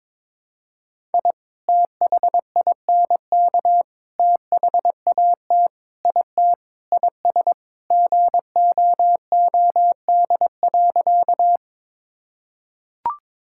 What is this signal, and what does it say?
Morse code, heard as a keyed tone, about 22 wpm, 700 Hz. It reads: I THINK THAT IT IS GOOD.
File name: practice / 22wpm / Sentences with 3-Word Phrases - 22wpm